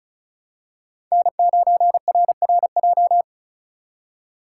Morse code, heard as a keyed tone, about 36 wpm, 700 Hz. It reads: N9RRJ
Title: USA Call Signs - 5 Characters - 2x Character Spacing - 35wpm